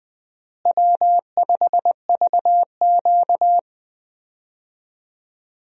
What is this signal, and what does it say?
Morse code, heard as a keyed tone, about 20 wpm, 700 Hz. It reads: W5VQ
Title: USA Call Signs - 4 Characters - Repeated 3x - 20wpm